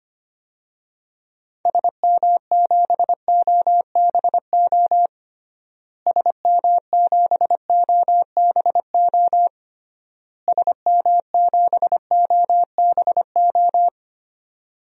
Morse code, heard as a keyed tone, about 25 wpm, 700 Hz. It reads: SM7OBO SM7OBO SM7OBO
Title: DXCC-100 Call Signs - 6 Characters - Repeated 3x - 25wpm